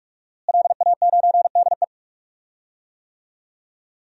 Morse code, heard as a keyed tone, about 42 wpm, 700 Hz. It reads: LA9DE
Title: DXCC-100 Call Signs - 5 Characters - Repeated 2x - 45wpm